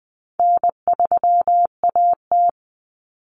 Morse code, heard as a keyed tone, about 20 wpm, 700 Hz. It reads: N3AT
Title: Common Cut Numbers - 4-Digit 20wpm